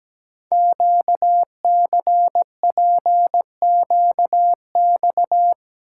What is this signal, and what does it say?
Morse code, heard as a keyed tone, about 17 wpm, 700 Hz. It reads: QCPQX
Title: Code Groups - 5 Letters - 17wpm